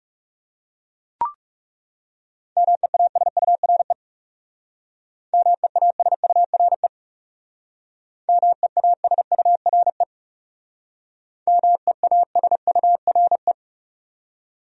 Morse code, heard as a keyed tone, about 37 wpm, 700 Hz. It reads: MEASURE MEASURE MEASURE MEASURE
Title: Speed-Racing Top 500 Words 30wpm